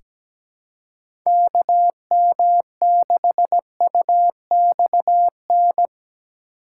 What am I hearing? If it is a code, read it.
Morse code, heard as a keyed tone, about 17 wpm, 700 Hz. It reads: KM6UXN
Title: USA Call Signs - 6 Characters - 2x Character Spacing - 17wpm